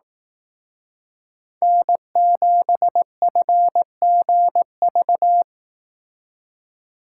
Morse code, heard as a keyed tone, about 18 wpm, 700 Hz. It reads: N7FGV
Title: Speed Racing - USA Call Signs 15wpm